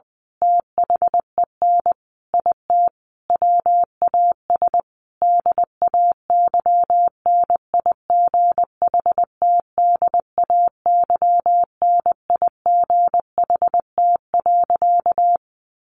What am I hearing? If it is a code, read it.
Morse code, heard as a keyed tone, about 20 wpm, 700 Hz. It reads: THEN IT WAS DAYNIGHTDAYNIGHT.
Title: A Sound Of Thunder 20wpm